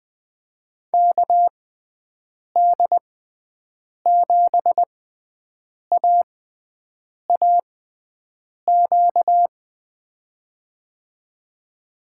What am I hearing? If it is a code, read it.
Morse code, heard as a keyed tone, about 20 wpm, spaced as for 7 wpm, 700 Hz. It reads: KD7AAQ
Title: USA Call Signs - 6x Character Spacing - 20wpm